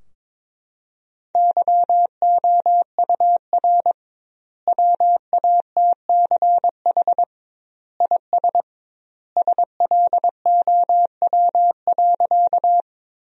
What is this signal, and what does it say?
Morse code, heard as a keyed tone, about 22 wpm, 700 Hz. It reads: YOUR WATCH IS SLOW.
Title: 2x Word-Spacing - Sentences from Top 400 Words - 22wpm